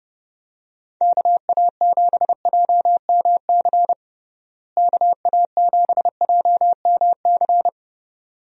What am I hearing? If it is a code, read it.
Morse code, heard as a keyed tone, about 30 wpm, 700 Hz. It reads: KA7JMC KA7JMC
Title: USA Call Signs - 6 Characters - Repeated 2x - 30wpm